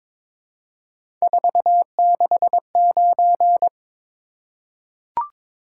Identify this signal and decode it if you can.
Morse code, heard as a keyed tone, about 22 wpm, 700 Hz. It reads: 469
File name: practice / 22wpm / Three Letter-Number Combinations - 22wpm